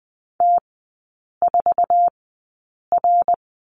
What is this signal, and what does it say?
Morse code, heard as a keyed tone, about 20 wpm, spaced as for 8 wpm, 700 Hz. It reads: T4R
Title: 067 Lesson 12e - Letter d - Three Characters 20wpm